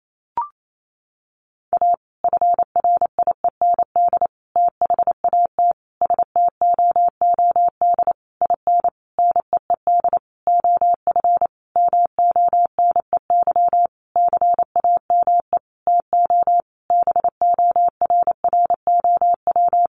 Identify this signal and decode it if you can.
Morse code, heard as a keyed tone, about 28 wpm, 700 Hz. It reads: A FRIEND THAT STOOD IN NEED OF MONEY CAME TO BORROW
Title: Sentences from Top 1500 Words 28wpm